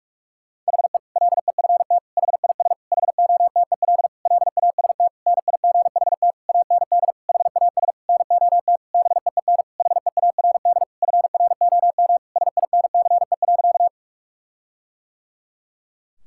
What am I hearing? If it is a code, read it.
Morse code, heard as a keyed tone, about 41 wpm, 700 Hz. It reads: HE LEFT HIS HOTEL LAST NIGHT AND HAS NOT BEEN HEARD FROM SINCE.
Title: Sentences from Top 1600 Words 45wpm